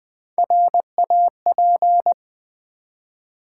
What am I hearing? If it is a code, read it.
Morse code, heard as a keyed tone, about 20 wpm, 700 Hz. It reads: RAP
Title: Build A Word - 1 Letter At A Time - 4-Letter Words - 20wpm